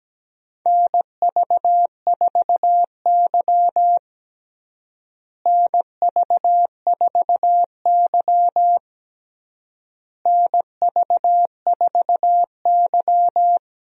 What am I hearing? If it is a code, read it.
Morse code, heard as a keyed tone, about 17 wpm, 700 Hz. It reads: NV4Y NV4Y NV4Y
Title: USA Call Signs - 4 Characters - Repeated 3x - 17wpm